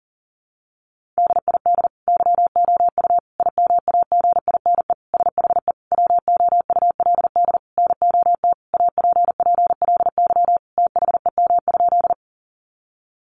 Morse code, heard as a keyed tone, about 40 wpm, 700 Hz. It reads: DID YOU IMAGINE SHE WOULD NOT APPLY THEM?
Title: Sentences from Top 1400 Words 40wpm